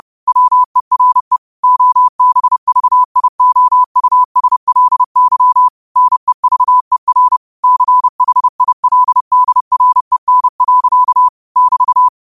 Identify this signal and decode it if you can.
Morse code, heard as a keyed tone, about 30 wpm, 1000 Hz. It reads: WERE OBVIOUSLY NEVER CHILDREN. <BT>
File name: qotdfiles / 30wpm